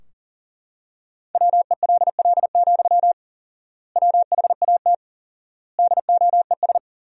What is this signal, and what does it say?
Morse code, heard as a keyed tone, about 40 wpm, 700 Hz. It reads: WELL, WHAT DOES
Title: 4x Word-Spacing - Sentences from Top 500 Words - 40wpm